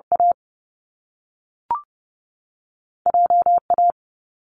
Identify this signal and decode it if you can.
Morse code, heard as a keyed tone, about 30 wpm, 700 Hz. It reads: A J A
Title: Top Two Letter Combinations in Words 30wpm